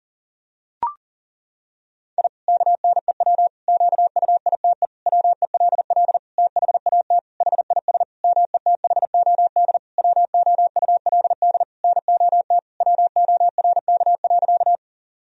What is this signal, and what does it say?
Morse code, heard as a keyed tone, about 37 wpm, 700 Hz. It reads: I KNEW QUITE WELL THAT HIS METHOD WOULD NOT WORK.
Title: Sentences from Top 1000 Words - Review Part 3 - 40wpm